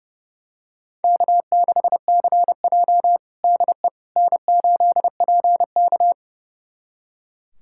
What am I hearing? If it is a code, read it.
Morse code, heard as a keyed tone, about 30 wpm, 700 Hz. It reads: K6CJ DE N8PK
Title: USA Callsign DE Callsign - 4 Characters - 30wpm